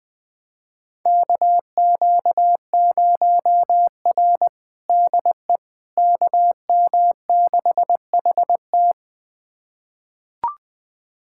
Morse code, heard as a keyed tone, about 21 wpm, 700 Hz. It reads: KQ0R DE KM6HT
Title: USA Callsign DE Callsign 20wpm